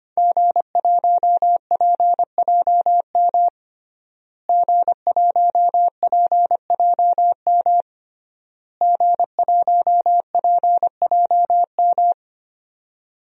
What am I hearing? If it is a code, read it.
Morse code, heard as a keyed tone, about 25 wpm, 700 Hz. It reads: G1PJM G1PJM G1PJM
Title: DXCC-100 Call Signs - 5 Characters - Repeated 3x - 25wpm